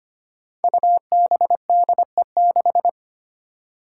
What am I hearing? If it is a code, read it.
Morse code, heard as a keyed tone, about 25 wpm, 700 Hz. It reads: UBDE6
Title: Five Challenging Letter-Number Combinations 25wpm